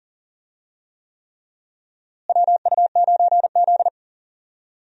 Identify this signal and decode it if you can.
Morse code, heard as a keyed tone, about 37 wpm, 700 Hz. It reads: WU9Z
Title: USA Call Signs - 4 Characters - Repeated 3x - 40wpm